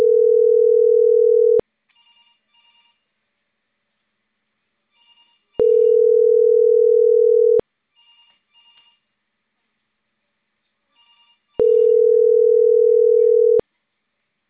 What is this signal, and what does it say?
Ringback tone (North America), 440 + 480 Hz.